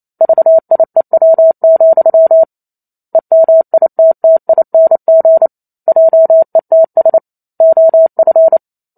Morse code, heard as a keyed tone, about 28 wpm, 650 Hz. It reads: VIEW, EMITTING JETS OF